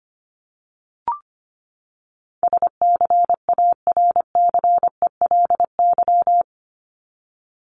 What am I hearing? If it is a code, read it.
Morse code, heard as a keyed tone, about 25 wpm, 700 Hz. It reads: SCARCELY
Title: LY Suffix - Top 100 Words with Base Words 25wpm